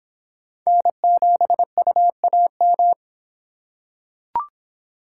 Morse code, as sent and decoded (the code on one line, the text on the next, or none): -. --... ..- .- --
N7UAM